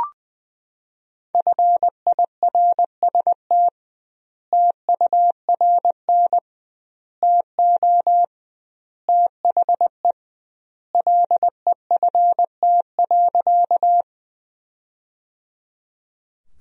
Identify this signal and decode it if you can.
Morse code, heard as a keyed tone, about 21 wpm, 700 Hz. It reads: FIRST TURN TO THE LEFT.
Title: 2x Word-Spacing - Sentences from Top 300 Words - 20wpm